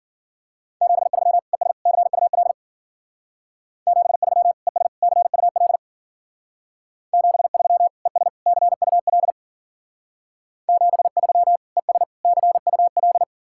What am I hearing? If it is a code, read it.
Morse code, heard as a keyed tone, about 51 wpm, 700 Hz. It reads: 73 ES CUL 73 ES CUL 73 ES CUL 73 ES CUL